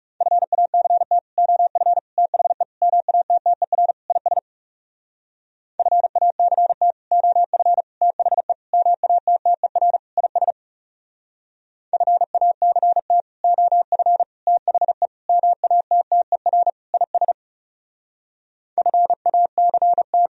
Morse code, heard as a keyed tone, about 36 wpm, 700 Hz. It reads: FACT OF THE MATTER IS FACT OF THE MATTER IS FACT OF THE MATTER IS FACT
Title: Speed-Racing - Sets of 5 Words 30wpm